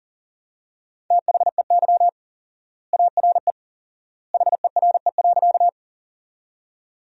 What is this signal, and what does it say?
Morse code, heard as a keyed tone, about 40 wpm, 700 Hz. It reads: THEY ARE HERE.
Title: 4x Word-Spacing - Sentences from Top 200 Words - 40wpm